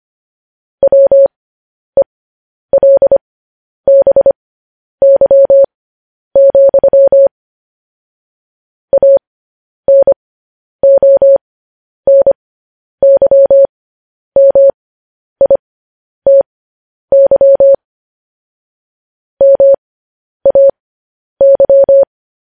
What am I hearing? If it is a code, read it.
Morse code, heard as a keyed tone, about 25 wpm, spaced as for 10 wpm, 550 Hz. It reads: WELBY, ANONYMITY MAY
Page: news-headlines-in-morse-code-at-10-wpm